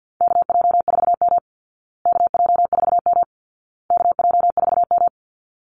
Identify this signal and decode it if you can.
Morse code, heard as a keyed tone, about 50 wpm, 700 Hz. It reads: KJ4M KJ4M KJ4M
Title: Call Signs Repeated 2x - 50wpm